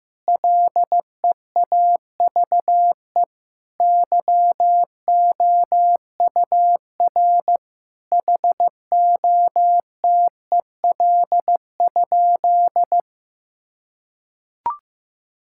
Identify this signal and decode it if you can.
Morse code, heard as a keyed tone, about 16 wpm, 700 Hz. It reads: LEAVE YOUR HOTEL?
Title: Sentences from Top 1600 Words 15wpm